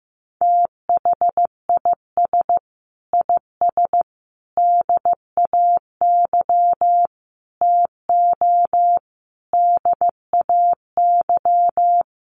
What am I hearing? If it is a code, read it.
Morse code, heard as a keyed tone, about 15 wpm, 700 Hz. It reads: THIS IS DAY TO DAY
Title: Sets of 5 Words Encore 15wpm